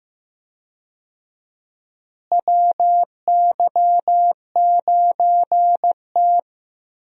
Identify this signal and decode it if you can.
Morse code, heard as a keyed tone, about 15 wpm, 700 Hz. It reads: WY9T
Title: USA Call Signs - 4 Characters - Repeated 3x - 15wpm